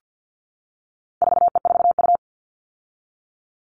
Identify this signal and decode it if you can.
Morse code, heard as a keyed tone, about 50 wpm, 700 Hz. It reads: 4E4U
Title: All Cut Numbers - 4-Digit 50wpm